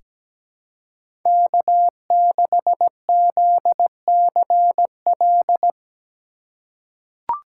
Morse code, heard as a keyed tone, about 17 wpm, 700 Hz. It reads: K6ZCL
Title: K1USN SST QSO Practice - 17wpm